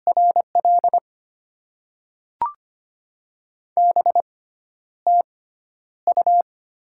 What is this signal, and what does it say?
Morse code, heard as a keyed tone, about 25 wpm, spaced as for 11 wpm, 700 Hz. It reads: RL BTU
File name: Top 20 QSO Elements - 6x Character Spacing - 25wpm